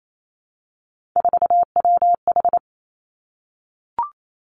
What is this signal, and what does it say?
Morse code, heard as a keyed tone, about 28 wpm, 700 Hz. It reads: 4WH
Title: Three Challenging Letter-Number Combinations 28wpm